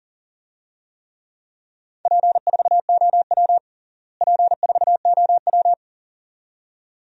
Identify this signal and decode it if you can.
Morse code, heard as a keyed tone, about 40 wpm, 700 Hz. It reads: P4OW P4OW
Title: DXCC-100 Call Signs - 4 Characters - Repeated 2x - 40wpm